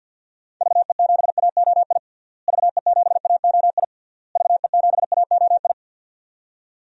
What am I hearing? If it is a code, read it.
Morse code, heard as a keyed tone, about 46 wpm, 700 Hz. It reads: VE7AOI VE7AOI VE7AOI
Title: DXCC-100 Call Signs - 6 Characters - Repeated 3x - 50wpm